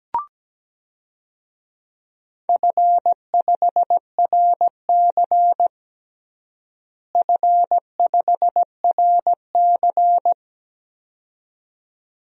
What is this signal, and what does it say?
Morse code, heard as a keyed tone, about 17 wpm, 700 Hz. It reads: F5RC F5RC
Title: DXCC-100 Call Signs - 4 Characters - Repeated 2x - 17wpm